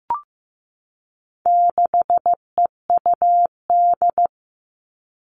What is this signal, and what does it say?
Morse code, heard as a keyed tone, about 15 wpm, 700 Hz. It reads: 6EUD
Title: All Cut Numbers - 4-Digit 15wpm